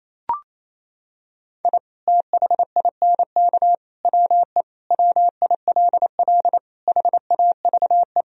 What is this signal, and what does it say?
Morse code, heard as a keyed tone, about 28 wpm, 700 Hz. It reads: I THINK WE WILL HAVE
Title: Sentences from Top 1000 Words - Review Part 1 - 28wpm